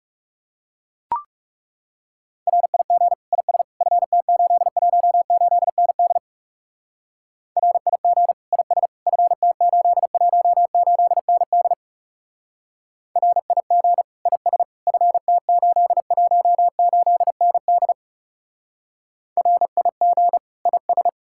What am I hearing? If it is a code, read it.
Morse code, heard as a keyed tone, about 37 wpm, 700 Hz. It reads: RIG IS FT818ND RIG IS FT818ND RIG IS FT818ND RIG IS
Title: Speed-Racing Common QSO Phrases - 30wpm